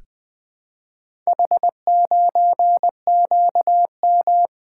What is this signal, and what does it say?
Morse code, heard as a keyed tone, about 20 wpm, 700 Hz. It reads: H9QM
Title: Four Letter-Number Combinations - 20wpm